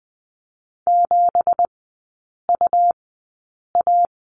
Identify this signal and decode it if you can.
Morse code, heard as a keyed tone, about 20 wpm, spaced as for 8 wpm, 700 Hz. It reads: 7UA
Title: 166 Lesson 27e - Number 7 - Three Characters 20wpm